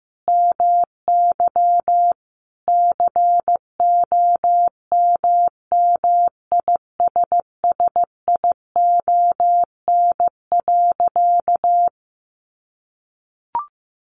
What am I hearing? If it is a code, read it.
Morse code, heard as a keyed tone, about 14 wpm, 700 Hz. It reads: MY COMMISSION.
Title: Sentences from Top 1400 Words 15wpm